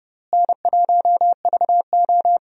Morse code, heard as a keyed tone, about 30 wpm, 700 Hz. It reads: N1VO
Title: USA Call Signs - 4 Characters - Repeated 2x - 30wpm